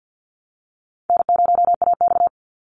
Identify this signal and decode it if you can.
Morse code, heard as a keyed tone, about 48 wpm, 700 Hz. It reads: N0AX
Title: Easy Call Signs Repeated 2x - 50wpm